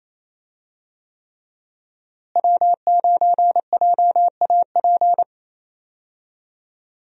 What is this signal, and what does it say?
Morse code, heard as a keyed tone, about 28 wpm, 700 Hz. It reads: W9JAP
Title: USA Call Signs - 5 Characters - Repeated 3x - 28wpm